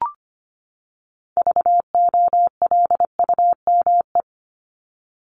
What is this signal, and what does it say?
Morse code, heard as a keyed tone, about 25 wpm, 700 Hz. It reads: VOLUME
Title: Top 1751-2000 Words 25wpm